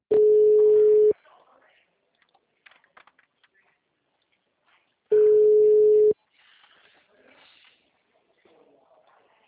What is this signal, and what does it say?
Ringback tone (Europe, 425 Hz), with speech.